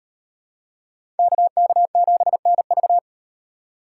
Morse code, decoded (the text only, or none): KK7NV